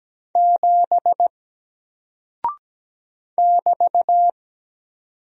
Morse code, heard as a keyed tone, about 17 wpm, 700 Hz. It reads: 7 <BT>
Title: Single Character (Letter:Number:Punctuation:Prosign) 17wpm